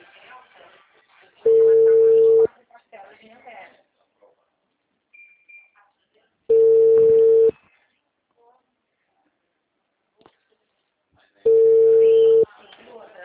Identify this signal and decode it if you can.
Ringback tone (China, 450 Hz), with speech.